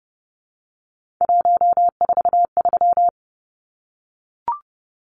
Morse code, heard as a keyed tone, about 30 wpm, 700 Hz. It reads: 143
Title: Numbers - 3-Digit 30wpm